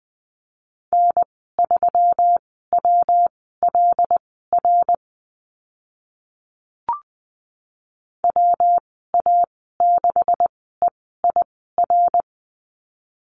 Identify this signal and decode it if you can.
Morse code, heard as a keyed tone, about 20 wpm, spaced as for 15 wpm, 700 Hz. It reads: N3WLR WA6EIR
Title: 173 Lesson 28g - Number 9 - Call Signs 20wpm